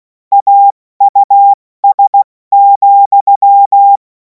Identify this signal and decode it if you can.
Morse code, heard as a keyed tone, about 16 wpm, spaced as for 14 wpm, 800 Hz. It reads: AUS,
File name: Morsekurs / Witze / Witz 31. März 2026 70 F80